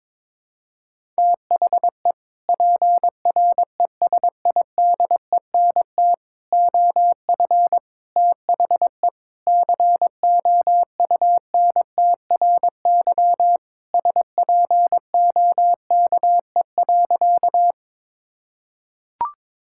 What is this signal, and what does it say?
Morse code, heard as a keyed tone, about 22 wpm, 700 Hz. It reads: THE PRESIDENT OF THE COUNTRY SPOKE.
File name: practice / 22wpm / Sentences with 3-Word Phrases - 22wpm